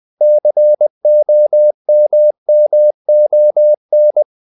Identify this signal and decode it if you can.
Morse code, heard as a keyed tone, about 20 wpm, 600 Hz. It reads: COMMON
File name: CW / Practice / If20